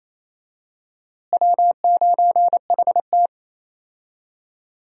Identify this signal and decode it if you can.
Morse code, heard as a keyed tone, about 28 wpm, 700 Hz. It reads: W9HT
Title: USA Call Signs - 4 Characters - Repeated 3x - 28wpm